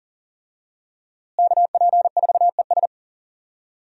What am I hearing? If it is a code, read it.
Morse code, heard as a keyed tone, about 38 wpm, 700 Hz. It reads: KP4ES
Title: USA Call Signs - 5 Characters - Repeated 3x - 40wpm